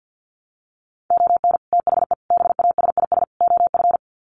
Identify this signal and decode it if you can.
Morse code, heard as a keyed tone, about 50 wpm, 700 Hz. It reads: ON THE BASIS OF
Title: Sets of 4 Words 50wpm